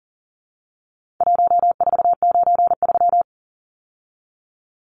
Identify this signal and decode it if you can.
Morse code, heard as a keyed tone, about 40 wpm, 700 Hz. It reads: 1493
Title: Numbers - Contest Style - 4-Digit 40wpm